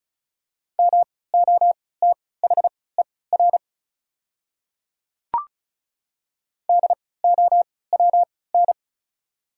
Morse code, heard as a keyed tone, about 34 wpm, spaced as for 20 wpm, 690 Hz. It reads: MOTHER DOWN
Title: 3x Character Spacing - Top 200 Words - 35wpm